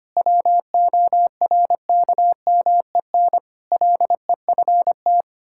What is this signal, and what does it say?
Morse code, heard as a keyed tone, about 25 wpm, 700 Hz. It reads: WORKMEN LEFT
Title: Pangram Challenge v2 - 25wpm